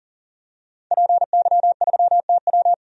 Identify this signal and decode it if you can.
Morse code, heard as a keyed tone, about 40 wpm, 700 Hz. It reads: PY3TW
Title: DXCC-100 Call Signs - 5 Characters - 40wpm